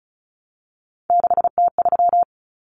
Morse code, heard as a keyed tone, about 34 wpm, 700 Hz. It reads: 6T3
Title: Common Cut Numbers - 3-Digit 35wpm